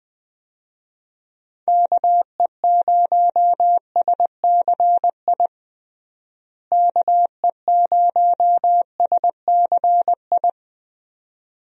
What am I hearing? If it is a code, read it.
Morse code, heard as a keyed tone, about 20 wpm, 700 Hz. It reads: KE0SCI KE0SCI
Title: USA Call Signs - 6 Characters - Repeated 2x - 20wpm